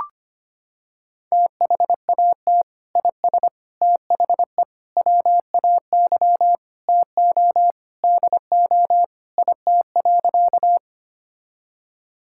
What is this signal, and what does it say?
Morse code, heard as a keyed tone, about 25 wpm, 700 Hz. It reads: THAT IS THE WAY TO DO IT.
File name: practice / 25wpm / Sentences with 3-Word Phrases - 25wpm